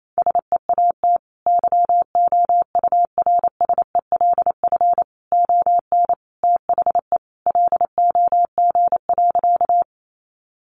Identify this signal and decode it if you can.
Morse code, heard as a keyed tone, about 28 wpm, 700 Hz. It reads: SEAT YOURSELF ON THE LOG.
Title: Sentences from Top 1500 Words 28wpm